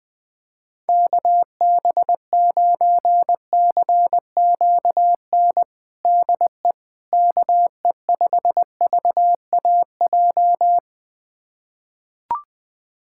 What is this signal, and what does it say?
Morse code, heard as a keyed tone, about 20 wpm, 700 Hz. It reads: KB9CQN DE KE5VAJ
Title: USA Callsign DE Callsign - 6 Characters - 20wpm